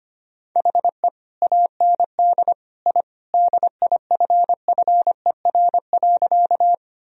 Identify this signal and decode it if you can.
Morse code, heard as a keyed tone, about 25 wpm, 700 Hz. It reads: HE AND I DIFFER.
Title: Sentences from Top 1000 Words - Review Part 1 - 25wpm